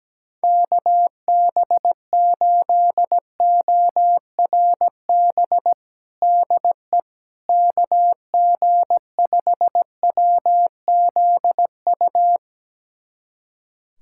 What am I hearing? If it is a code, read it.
Morse code, heard as a keyed tone, about 17 wpm, 700 Hz. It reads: KB8ORB DE KG5WZU
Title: USA Callsign DE Callsign - 6 Characters - 17wpm